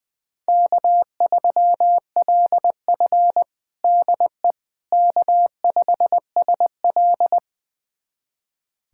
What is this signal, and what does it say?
Morse code, heard as a keyed tone, about 20 wpm, 700 Hz. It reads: K3LF DE K5SL
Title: USA Callsign DE Callsign - 4 Characters - 20wpm